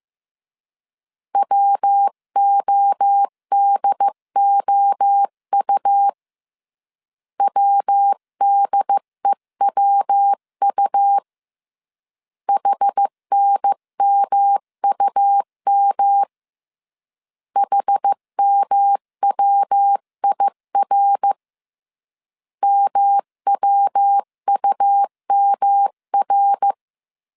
Morse code, heard as a keyed tone, about 15 wpm, 800 Hz. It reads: WODOU WDEWU HNMUM HMWIR MWUMR
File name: CW / DET / 1 3 1500C S A 14wpm 2447